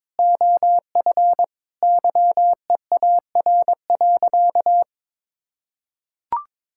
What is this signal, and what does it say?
Morse code, heard as a keyed tone, about 22 wpm, 700 Hz. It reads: OF YEAR.
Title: Sentences from Top 1000 Words - Review Part 6 - 22wpm